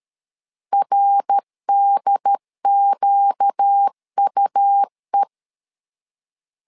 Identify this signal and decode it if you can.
Morse code, heard as a keyed tone, about 13 wpm, 800 Hz. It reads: RDQUE
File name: LEZIONI / 4 Q 12wpm 1833